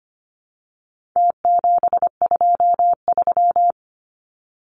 Morse code, heard as a keyed tone, about 25 wpm, 700 Hz. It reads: T723
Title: Numbers - Contest Style - 4-Digit 25wpm